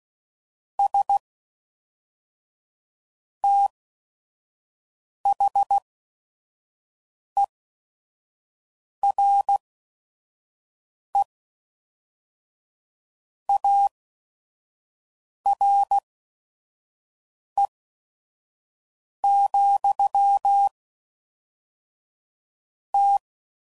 Morse code, heard as a keyed tone, about 16 wpm, spaced as for 5 wpm, 800 Hz. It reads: S THERE ARE, T
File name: CW Code Groups @5WPM